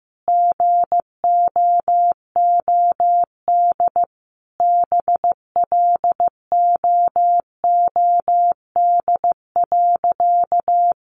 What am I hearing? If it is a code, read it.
Morse code, heard as a keyed tone, about 15 wpm, 700 Hz. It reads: GOOD BLOOD.
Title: Sentences from Top 1100 Words 15wpm